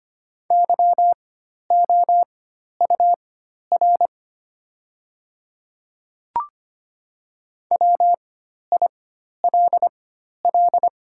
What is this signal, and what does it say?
Morse code, heard as a keyed tone, about 25 wpm, spaced as for 12 wpm, 700 Hz. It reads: YOUR WILL
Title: 4x Character Spacing - Top 100 Words - 25wpm